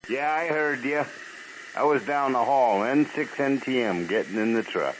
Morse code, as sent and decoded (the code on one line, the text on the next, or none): none